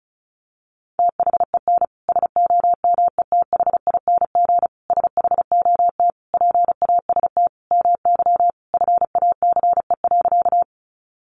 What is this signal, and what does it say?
Morse code, heard as a keyed tone, about 34 wpm, 700 Hz. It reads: THEN SOMETHING SHOT PAST MY FACE.
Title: Sentences from Top 1100 Words 35wpm